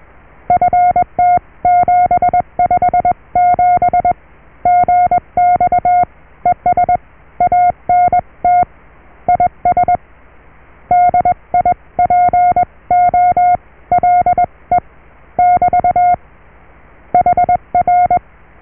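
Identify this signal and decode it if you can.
Morse code, heard as a keyed tone, about 21 wpm, spaced as for 18 wpm, 710 Hz. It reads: FT757GX ES ANT IS DIPOLE <BT> HR
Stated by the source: Morse QSO on 14 MHz with a German station, received at RST 579.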